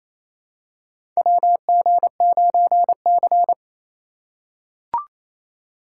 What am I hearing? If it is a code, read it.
Morse code, heard as a keyed tone, about 28 wpm, 700 Hz. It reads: WG9C
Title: USA Call Signs - 4 Characters - 4x Character Spacing - 28wpm